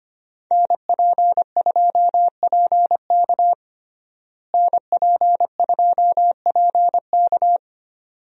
Morse code, heard as a keyed tone, about 25 wpm, 700 Hz. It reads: NP2PK NP2PK
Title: USA Call Signs - 5 Characters - Repeated 2x - 25wpm